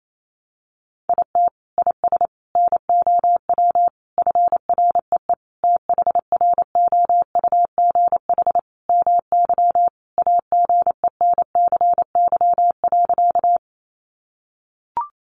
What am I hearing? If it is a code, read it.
Morse code, heard as a keyed tone, about 28 wpm, 700 Hz. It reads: IT IS NOW FREE THROUGH MY AGENCY.
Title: Sentences from Top 1300 Words 28wpm